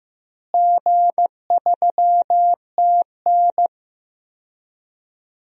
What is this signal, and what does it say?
Morse code, heard as a keyed tone, about 16 wpm, 700 Hz. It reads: G3TN
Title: DXCC-100 Call Signs - 4 Characters - Repeated 3x - 15wpm